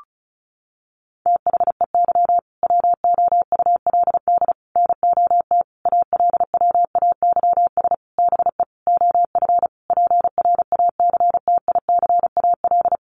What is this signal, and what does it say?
Morse code, heard as a keyed tone, about 34 wpm, 700 Hz. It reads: THEY WOULD NOT ALWAYS BE OF PRACTICAL
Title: Sentences from Top 1400 Words 35wpm